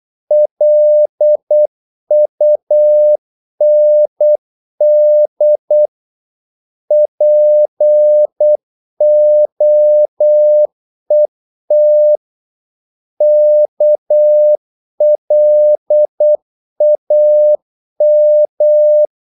Morse code, heard as a keyed tone, about 8 wpm, 600 Hz. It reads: LUND POET KLAM
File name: Korte ord DK 8wpm